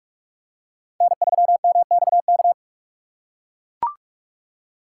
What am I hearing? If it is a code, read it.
Morse code, heard as a keyed tone, about 43 wpm, 700 Hz. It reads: N3MXK
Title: USA Call Signs - 5 Characters - 3x Character Spacing - 45wpm